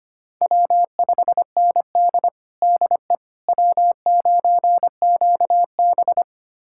Morse code, heard as a keyed tone, about 25 wpm, 700 Hz. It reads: W5ND DE W9QB
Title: USA Callsign DE Callsign - 4 Characters - 25wpm